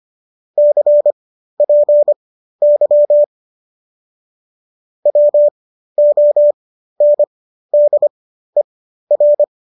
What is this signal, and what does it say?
Morse code, heard as a keyed tone, about 25 wpm, spaced as for 13 wpm, 590 Hz. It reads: CPY WONDER